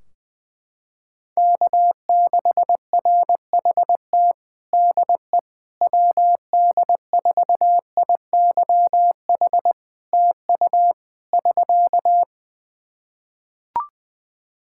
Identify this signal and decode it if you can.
Morse code, heard as a keyed tone, about 21 wpm, 700 Hz. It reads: K6RHT DE WD4IYH TU <SK>
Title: Practice QSO - Exchange 4 - Ending - 20wpm